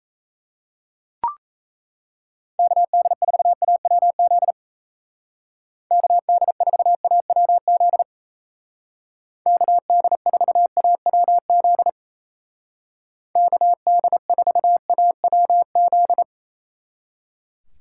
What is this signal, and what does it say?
Morse code, heard as a keyed tone, about 34 wpm, 700 Hz. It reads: KD4AWZ KD4AWZ KD4AWZ KD4AWZ
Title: Speed Racing - USA 2x3 Call Signs 28wpm